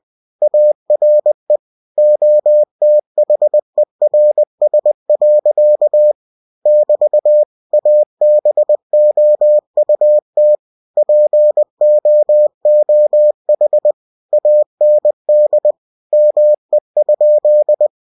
Morse code, heard as a keyed tone, about 20 wpm, 600 Hz. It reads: ARE OTHERS. <BT> ABOUT POOH AND ME?
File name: cw-books / winnie-the-pooh / winnie-20wpm-page-0011